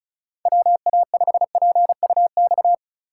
Morse code, heard as a keyed tone, about 44 wpm, 700 Hz. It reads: WA5PUX